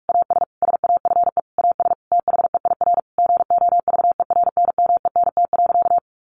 Morse code, heard as a keyed tone, about 44 wpm, 700 Hz. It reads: AS SAFE AS THEIR GOVERNMENT.
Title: Sentences from Top 1500 Words 45wpm